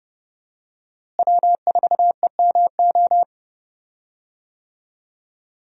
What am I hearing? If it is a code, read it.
Morse code, heard as a keyed tone, about 30 wpm, 700 Hz. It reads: W4EMO
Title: USA Call Signs - 5 Characters - Repeated 2x - 30wpm